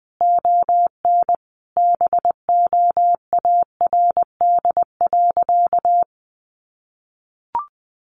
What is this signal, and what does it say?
Morse code, heard as a keyed tone, about 20 wpm, 700 Hz. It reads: ON BOARD.
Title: Sentences from Top 700 Words 20wpm